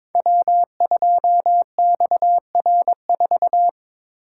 Morse code, heard as a keyed tone, about 21 wpm, 700 Hz. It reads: W2XR4
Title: Code Groups - 5 Letter:Numbers - 22wpm